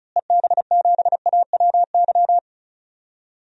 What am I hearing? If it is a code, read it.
Morse code, heard as a keyed tone, about 36 wpm, 700 Hz. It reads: EB7AWY
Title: DXCC-100 Call Signs - 6 Characters - Repeated 2x - 35wpm